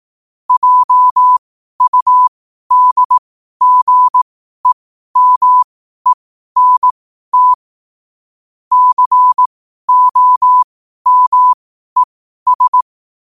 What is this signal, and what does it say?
Morse code, heard as a keyed tone, about 18 wpm, spaced as for 12 wpm, 1000 Hz. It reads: JUDGEMENT COMES